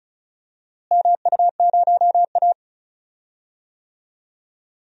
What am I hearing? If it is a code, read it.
Morse code, heard as a keyed tone, about 36 wpm, 700 Hz. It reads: MU0A
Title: DXCC-100 Call Signs - 4 Characters - Repeated 3x - 35wpm